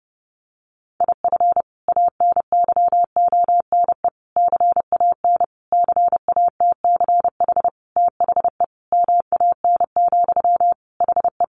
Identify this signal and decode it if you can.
Morse code, heard as a keyed tone, about 30 wpm, 700 Hz. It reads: IF ANYONE CAN CATCH THE MAN, HE
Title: Sentences from Top 1500 Words 30wpm